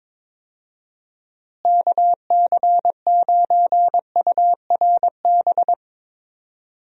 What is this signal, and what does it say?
Morse code, heard as a keyed tone, about 22 wpm, 700 Hz. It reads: KC9URB
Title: USA Call Signs - 6 Characters - Repeated 2x - 22wpm